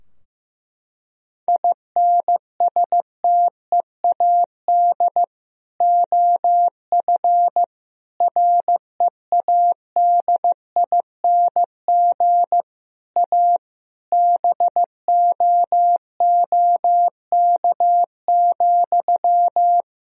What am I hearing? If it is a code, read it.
Morse code, heard as a keyed tone, about 15 wpm, 700 Hz. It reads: INSTEAD OF READING A BOOK,
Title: Sentences from Top 1000 Words - Review Part 4 - 15wpm